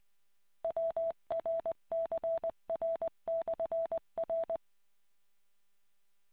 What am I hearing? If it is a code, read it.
Morse code, heard as a keyed tone, about 20 wpm, 660 Hz. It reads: WRCR/R